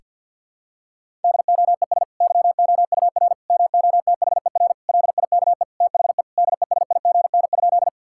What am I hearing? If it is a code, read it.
Morse code, heard as a keyed tone, about 52 wpm, 700 Hz. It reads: DOES YOUR MOTHER LIKE THE DESIGN?
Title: Sentences from Top 1000 Words - Review Part 5 - 50wpm